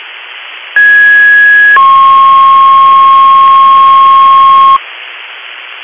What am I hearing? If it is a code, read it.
Two-tone page, heard as a one-second tone, then a three-second tone, about 1650 then 1080 Hz. It is heard from 0.8 to 4.8 s.